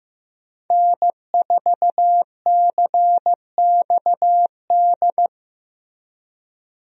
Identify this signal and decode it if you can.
Morse code, heard as a keyed tone, about 16 wpm, 700 Hz. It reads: N4CXD
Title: USA Call Signs - 5 Characters - Repeated 2x - 15wpm